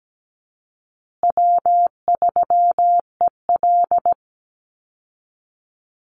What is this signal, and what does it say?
Morse code, heard as a keyed tone, about 17 wpm, 700 Hz. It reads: W3EL
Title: Call Signs Repeated 2x - 17wpm